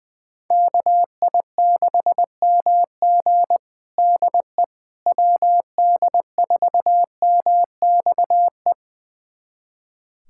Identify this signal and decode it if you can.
Morse code, heard as a keyed tone, about 21 wpm, 700 Hz. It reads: KI6MG DE WD4MXE
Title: USA Callsign DE Callsign 20wpm